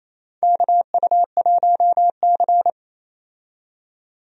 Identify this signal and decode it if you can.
Morse code, heard as a keyed tone, about 27 wpm, 700 Hz. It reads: KU1C
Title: USA Call Signs - 4 Characters - 28wpm